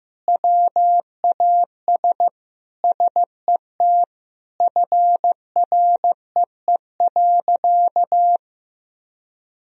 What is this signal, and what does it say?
Morse code, heard as a keyed tone, about 15 wpm, 700 Hz. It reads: WAS SET FREE.
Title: Sentences from Top 1000 Words - Review Part 1 - 15wpm